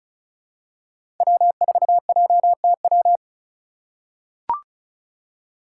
Morse code, heard as a keyed tone, about 36 wpm, 700 Hz. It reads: W4JTW
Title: USA Call Signs - 5 Characters - 2x Character Spacing - 35wpm